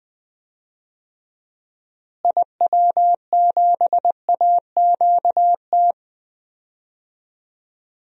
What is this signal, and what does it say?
Morse code, heard as a keyed tone, about 20 wpm, 700 Hz. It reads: IW7AQT
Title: DXCC-100 Call Signs - 6 Characters - Repeated 2x - 20wpm